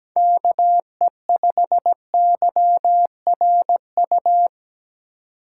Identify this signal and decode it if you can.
Morse code, heard as a keyed tone, about 17 wpm, 700 Hz. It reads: KE5YRU
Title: USA Call Signs - 6 Characters - 3x Character Spacing - 17wpm